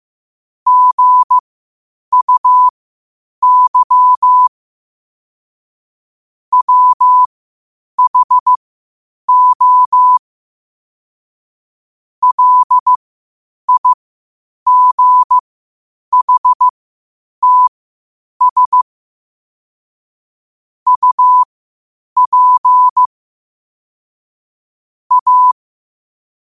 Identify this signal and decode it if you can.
Morse code, heard as a keyed tone, about 15 wpm, spaced as for 8 wpm, 1000 Hz. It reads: GUY WHO LIGHTS UP A
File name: qotdfiles / 8wpm